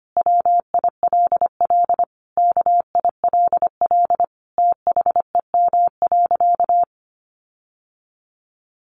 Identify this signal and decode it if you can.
Morse code, heard as a keyed tone, about 25 wpm, 700 Hz. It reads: WILL KILL THEM.
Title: Sentences from Top 1400 Words 25wpm